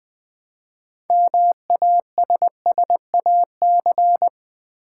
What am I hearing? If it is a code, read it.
Morse code, heard as a keyed tone, about 20 wpm, 700 Hz. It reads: MASSAC
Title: Build A Word - 3 Letters At A Time - 8-Letter Words - 20wpm